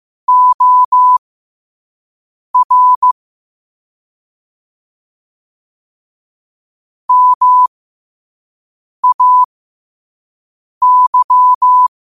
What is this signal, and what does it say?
Morse code, heard as a keyed tone, about 15 wpm, spaced as for 5 wpm, 1000 Hz. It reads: OR MAY
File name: qotdfiles / 5wpm